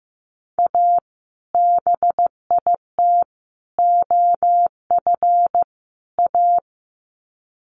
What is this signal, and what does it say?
Morse code, heard as a keyed tone, about 15 wpm, 700 Hz. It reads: A BIT OF A
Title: Sets of 4 Words 15wpm